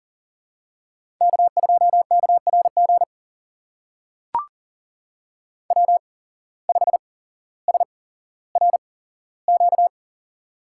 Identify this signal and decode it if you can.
Morse code, heard as a keyed tone, about 40 wpm, spaced as for 17 wpm, 700 Hz. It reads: K2KRG W5SRQ
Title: USA Call Signs - 5 Characters - 8x Character Spacing - 40wpm